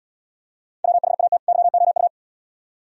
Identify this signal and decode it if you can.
Morse code, heard as a keyed tone, about 74 wpm, 700 Hz. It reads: WHAT YOU